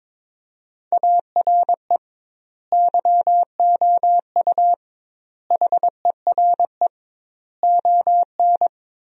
Morse code, heard as a keyed tone, about 22 wpm, 700 Hz. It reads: ARE YOU HERE ON
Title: 2x Word-Spacing - Sentences from Top 400 Words - 22wpm